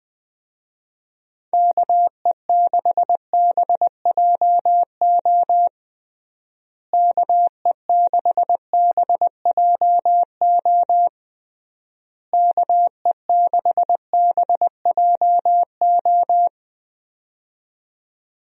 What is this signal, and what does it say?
Morse code, heard as a keyed tone, about 21 wpm, 700 Hz. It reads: KE6BJO KE6BJO KE6BJO
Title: USA Call Signs - 6 Characters - Repeated 3x - 20wpm